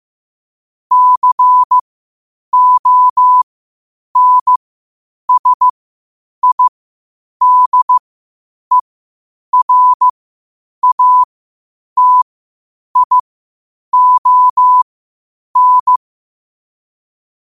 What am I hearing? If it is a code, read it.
Morse code, heard as a keyed tone, about 15 wpm, spaced as for 8 wpm, 1000 Hz. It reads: CONSIDERATION
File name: qotdfiles / 8wpm